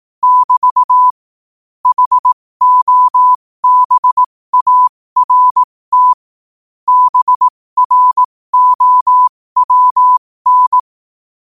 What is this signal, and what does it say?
Morse code, heard as a keyed tone, about 18 wpm, spaced as for 15 wpm, 1000 Hz. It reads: <BT> HOBART BROWN
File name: qotdfiles / 15wpm